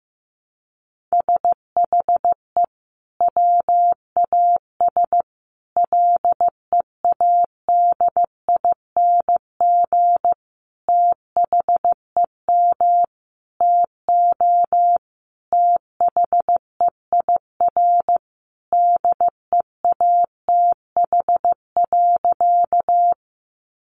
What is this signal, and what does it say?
Morse code, heard as a keyed tone, about 15 wpm, 700 Hz. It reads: SHE WAS LEADING THEM TO THEIR DEATH.
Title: Sentences from Top 1100 Words 15wpm